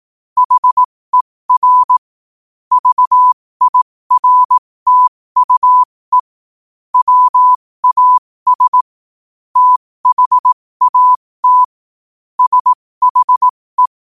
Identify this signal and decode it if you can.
Morse code, heard as a keyed tone, about 18 wpm, spaced as for 15 wpm, 1000 Hz. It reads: HER VIRTUE WAS THAT SHE
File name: qotdfiles / 15wpm